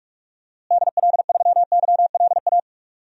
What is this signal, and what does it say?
Morse code, heard as a keyed tone, about 47 wpm, 700 Hz. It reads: DL3YLA
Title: DXCC-100 Call Signs - 6 Characters - Repeated 2x - 45wpm